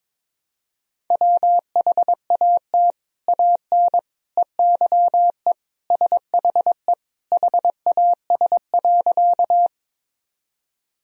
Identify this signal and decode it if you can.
Morse code, heard as a keyed tone, about 22 wpm, 700 Hz. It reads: WHAT AN EYE SHE HAS.
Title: Sentences from Top 1000 Words - Review Part 1 - 22wpm